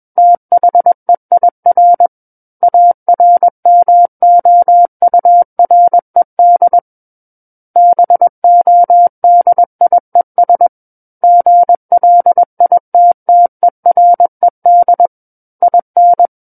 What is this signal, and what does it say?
Morse code, heard as a keyed tone, about 21 wpm, 700 Hz. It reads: THEIR ARMOURED BODIES GLITTERED IN
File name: Book One / 12-Chapter 12 - 21 WPM